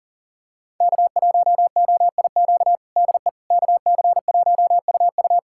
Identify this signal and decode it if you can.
Morse code, heard as a keyed tone, about 40 wpm, 700 Hz. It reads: K1OIQ DE KC1UU
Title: USA Callsign DE Callsign - 5 Characters - 40wpm